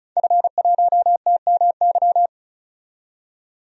Morse code, heard as a keyed tone, about 36 wpm, 700 Hz. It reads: F1TMY
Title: DXCC-100 Call Signs - 5 Characters - 35wpm